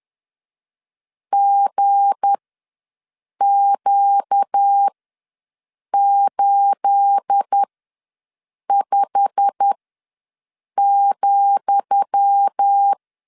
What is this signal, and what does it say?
Morse code, heard as a keyed tone, about 10 wpm, spaced as for 6 wpm, 800 Hz. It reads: GQ85,
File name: LEZIONI / 10 SIGNS / 10D COMA ws 10wpm 4314